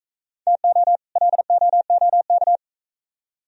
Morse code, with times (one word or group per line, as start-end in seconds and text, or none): 0.5-1.0 TO
1.2-2.6 LOOK